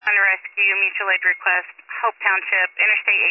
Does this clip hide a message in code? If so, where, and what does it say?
Yes: Emergency Alert System (SAME) tones from 0.1 to 2.6 s, heard as harsh buzzing bursts, with speech.